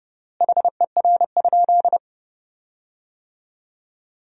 Morse code, heard as a keyed tone, about 32 wpm, 700 Hz. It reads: HER?
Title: 4x Word-Spacing - Sentences from Top 100 Words - 30wpm